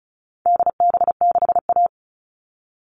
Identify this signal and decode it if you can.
Morse code, heard as a keyed tone, about 34 wpm, 700 Hz. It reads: DB6A
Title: All Cut Numbers - 4-Digit 35wpm